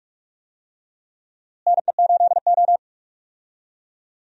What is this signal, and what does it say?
Morse code, heard as a keyed tone, about 46 wpm, 700 Hz. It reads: NE8O
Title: USA Call Signs - 4 Characters - Repeated 2x - 45wpm